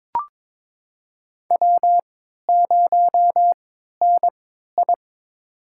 Morse code, heard as a keyed tone, about 23 wpm, spaced as for 13 wpm, 700 Hz. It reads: W0NI
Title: USA Call Signs - 3x Character Spacing - 22wpm